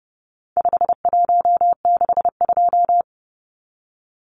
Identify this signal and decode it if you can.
Morse code, heard as a keyed tone, about 30 wpm, 700 Hz. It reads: 5162
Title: Numbers - 4-Digit 30wpm